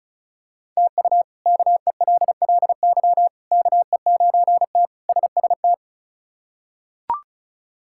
Morse code, heard as a keyed tone, about 34 wpm, 700 Hz. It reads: TU KELLY KE9T SST
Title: K1USN SST QSO Practice - 35wpm